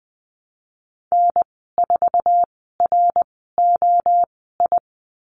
Morse code, heard as a keyed tone, about 20 wpm, spaced as for 14 wpm, 700 Hz. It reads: N4ROI
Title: 062 Lesson 11g - Letter h - Call Signs 20wpm